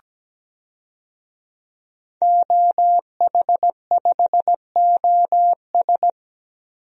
Morse code, heard as a keyed tone, about 17 wpm, 700 Hz. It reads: OH5OS
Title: DXCC-100 Call Signs - 5 Characters - Repeated 2x - 17wpm